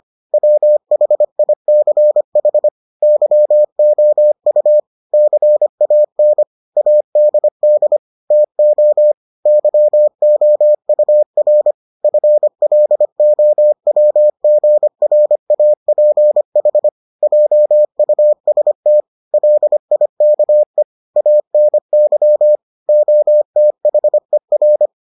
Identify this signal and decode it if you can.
Morse code, heard as a keyed tone, about 25 wpm, 600 Hz. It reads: WHICH YOU CAN ADD TO YOUR FLOWGRAPH JUST LIKE ANY OTHER